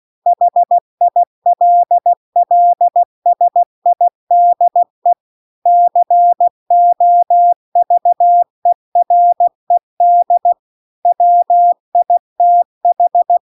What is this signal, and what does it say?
Morse code, heard as a keyed tone, about 16 wpm, 700 Hz. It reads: HILLSIDE COVERED WITH